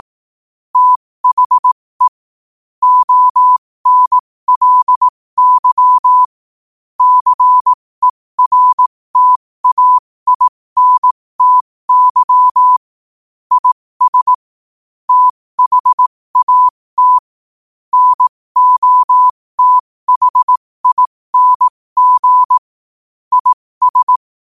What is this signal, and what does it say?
Morse code, heard as a keyed tone, about 18 wpm, spaced as for 15 wpm, 1000 Hz. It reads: THE ONLY CERTAINTY IS THAT NOTHING IS
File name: qotdfiles / 15wpm